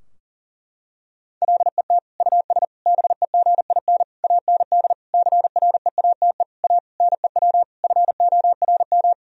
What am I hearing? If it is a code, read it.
Morse code, heard as a keyed tone, about 37 wpm, 700 Hz. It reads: LET US BEGIN AND CREATE A NEW FORM
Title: Sentences from Top 1000 Words - Review Part 3 - 40wpm